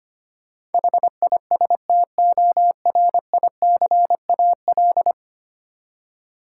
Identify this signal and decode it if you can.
Morse code, heard as a keyed tone, about 25 wpm, 700 Hz. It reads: HISTORICAL
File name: Ten Letter Words - 25wpm